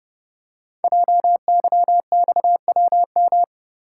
Morse code, heard as a keyed tone, about 30 wpm, 700 Hz. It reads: JYXWM
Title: Code Groups - 5 Letters - 30wpm